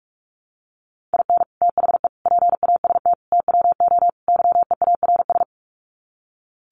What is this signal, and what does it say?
Morse code, heard as a keyed tone, about 44 wpm, 700 Hz. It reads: IN THE PAST TWO YEARS
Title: Sets of 5 Words 45wpm